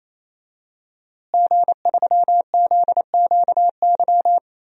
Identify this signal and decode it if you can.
Morse code, heard as a keyed tone, about 28 wpm, 700 Hz. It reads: G3ZQY
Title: DXCC-100 Call Signs - 5 Characters - Repeated 3x - 28wpm